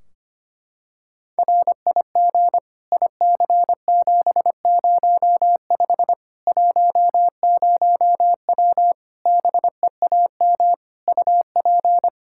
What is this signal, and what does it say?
Morse code, heard as a keyed tone, about 25 wpm, 700 Hz. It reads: RIG IC705 10W BEAM UP
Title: Practice QSO - Exchange 2 - Info - 25wpm